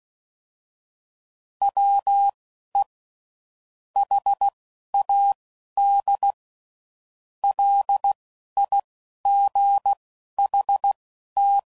Morse code, heard as a keyed tone, about 16 wpm, spaced as for 12 wpm, 800 Hz. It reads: WE HAD LIGHT
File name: CW Code Groups @10WPM / Weather Sets / Weather 4 24